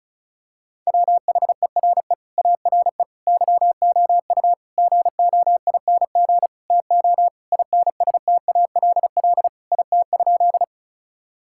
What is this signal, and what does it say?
Morse code, heard as a keyed tone, about 34 wpm, 700 Hz. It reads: WHERE ARE YOU GOING TO INSTALL IT?
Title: Sentences from Top 1600 Words 35wpm